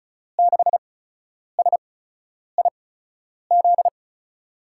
Morse code, heard as a keyed tone, about 36 wpm, spaced as for 10 wpm, 700 Hz. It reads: 6SIZ
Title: USA Call Signs - 6 Characters - 8x Character Spacing - 35wpm